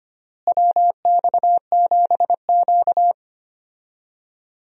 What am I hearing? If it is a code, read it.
Morse code, heard as a keyed tone, about 25 wpm, 700 Hz. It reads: WX7Q
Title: USA Call Signs - 4 Characters - Repeated 3x - 25wpm